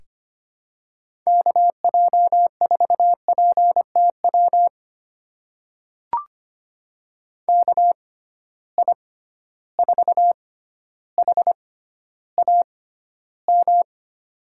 Morse code, heard as a keyed tone, about 25 wpm, spaced as for 13 wpm, 700 Hz. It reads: KJ4PTW KI4HAM
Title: USA Call Signs - 6x Character Spacing - 25wpm